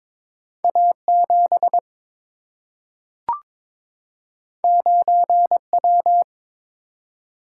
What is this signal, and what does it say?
Morse code, heard as a keyed tone, about 22 wpm, 700 Hz. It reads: A 7 9 W
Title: Single-Letter Single-Number 22wpm